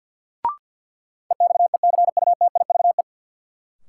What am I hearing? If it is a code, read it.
Morse code, heard as a keyed tone, about 53 wpm, 700 Hz. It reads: EXECUTIVE
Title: Sets of 5 Words Encore - New Word Drill - 50wpm